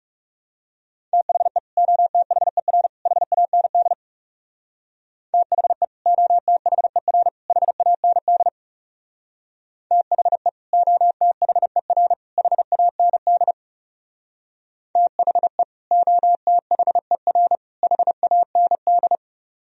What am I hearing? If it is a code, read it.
Morse code, heard as a keyed tone, about 38 wpm, 700 Hz. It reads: THE OTHER HAND THE OTHER HAND THE OTHER HAND THE OTHER HAND